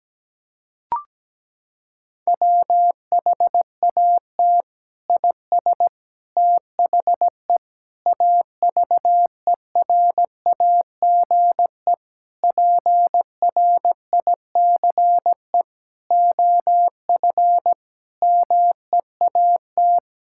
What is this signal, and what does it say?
Morse code, heard as a keyed tone, about 17 wpm, 700 Hz. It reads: WHAT IS THE AVERAGE PRICE OF MEAT
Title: Sentences from Top 1000 Words - Review Part 6 - 17wpm